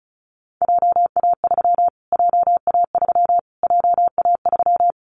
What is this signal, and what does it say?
Morse code, heard as a keyed tone, about 34 wpm, 700 Hz. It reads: JA3 JA3 JA3
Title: Callsign Prefixes for Top 25 DXCC - Repeated 3x - 35wpm